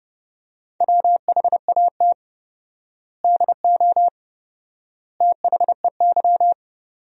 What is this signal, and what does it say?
Morse code, heard as a keyed tone, about 30 wpm, 700 Hz. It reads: WHAT DO THEY
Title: 4x Word-Spacing - Sentences from Top 100 Words - 30wpm